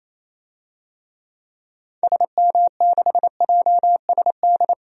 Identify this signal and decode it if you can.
Morse code, heard as a keyed tone, about 28 wpm, 700 Hz. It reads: SM6JSD
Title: DXCC-100 Call Signs - 6 Characters - 28wpm